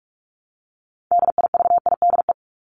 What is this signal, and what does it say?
Morse code, heard as a keyed tone, about 44 wpm, 700 Hz. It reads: DIVIDE
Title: ED Suffix - Top 100 Words with Base Words 45wpm